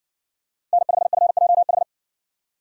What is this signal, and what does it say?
Morse code, heard as a keyed tone, about 60 wpm, 700 Hz. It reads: N5FJH